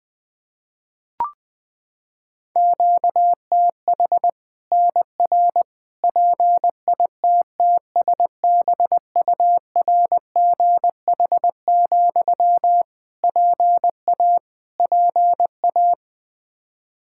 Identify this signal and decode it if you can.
Morse code, heard as a keyed tone, about 20 wpm, 700 Hz. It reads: QTH NR PITTSBURGH, PA PA
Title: Practice QSO - Exchange 1 - Beginning - 20wpm